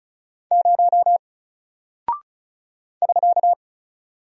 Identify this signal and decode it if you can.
Morse code, heard as a keyed tone, about 36 wpm, 700 Hz. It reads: 0 <SK>
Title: Single Character (Letter:Number:Punctuation:Prosign) 35wpm